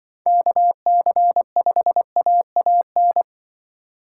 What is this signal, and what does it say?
Morse code, heard as a keyed tone, about 24 wpm, 700 Hz. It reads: KC5AAN
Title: Speed Racing - USA 2x3 Call Signs 20wpm